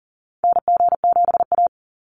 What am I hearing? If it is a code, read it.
Morse code, heard as a keyed tone, about 40 wpm, 700 Hz. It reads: NG7A
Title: NA SOTA Activator Callsigns 40wpm